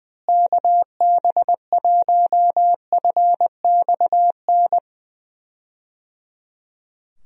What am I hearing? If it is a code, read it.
Morse code, heard as a keyed tone, about 21 wpm, 700 Hz. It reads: KB1FXN
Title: USA Call Signs - 6 Characters - Repeated 3x - 20wpm